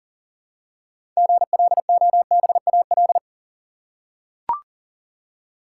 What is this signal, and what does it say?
Morse code, heard as a keyed tone, about 40 wpm, 700 Hz. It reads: GLOBAL
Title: Six Letter Words - 40wpm